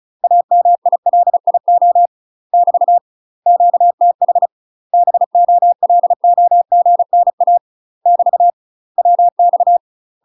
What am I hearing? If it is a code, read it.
Morse code, heard as a keyed tone, about 36 wpm, 700 Hz. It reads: AMILIO <BT> QTH BOLOGNA <BT> WX
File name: UK MASTER QSOs TRAINING / 35WPM / UK MASTER QSOs 0011